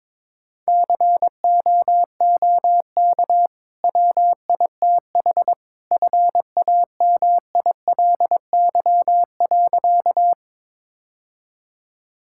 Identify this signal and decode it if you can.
Morse code, heard as a keyed tone, about 22 wpm, 700 Hz. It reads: COOK WITH FAMILY.
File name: practice / 22wpm / Easy Sentences with Hints - 22wpm